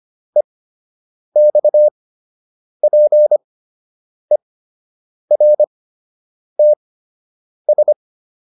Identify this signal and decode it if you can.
Morse code, heard as a keyed tone, about 25 wpm, spaced as for 8 wpm, 600 Hz. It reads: EXPERTS